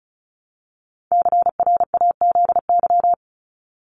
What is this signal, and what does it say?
Morse code, heard as a keyed tone, about 34 wpm, 700 Hz. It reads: CRAZY
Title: Top 1501-1750 Words 35wpm